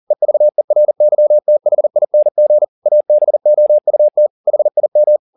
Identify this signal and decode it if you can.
Morse code, heard as a keyed tone, about 40 wpm, 600 Hz. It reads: EVERYTHING ABOUT HIM